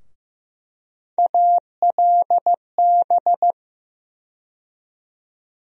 Morse code, heard as a keyed tone, about 15 wpm, 700 Hz. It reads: ALB
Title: USA Call Sign Postfixes - 3 Letters 15wpm